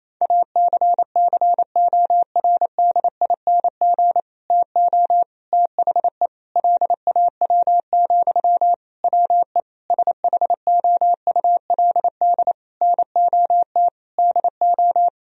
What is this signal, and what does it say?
Morse code, heard as a keyed tone, about 28 wpm, 700 Hz. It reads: ACCORDING TO THE LAW, WE SHOULD NOT DO
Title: Sentences from Top 1000 Words - Review Part 4 - 28wpm